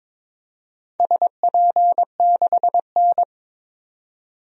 Morse code, heard as a keyed tone, about 22 wpm, 700 Hz. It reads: SP6N
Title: DXCC-100 Call Signs - 4 Characters - 22wpm